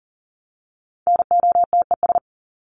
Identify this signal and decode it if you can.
Morse code, heard as a keyed tone, about 40 wpm, 700 Hz. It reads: NOTES
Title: ES:S Suffix - Top 100 Words with Base Words 40wpm